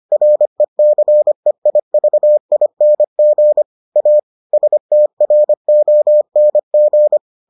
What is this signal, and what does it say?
Morse code, heard as a keyed tone, about 25 wpm, 600 Hz. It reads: RECEIVING A STRONG